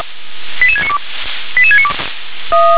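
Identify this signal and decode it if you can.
Five-tone selective call (ZVEI), heard as quick runs of five tones, tones of about 70 ms. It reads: 88692 88692
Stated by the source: ZVEI five-tone selective call 88776, received with an SDR.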